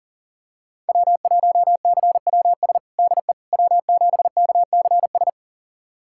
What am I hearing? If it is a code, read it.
Morse code, heard as a keyed tone, about 40 wpm, 700 Hz. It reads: W1CWS DE W7KCS
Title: USA Callsign DE Callsign - 5 Characters - 40wpm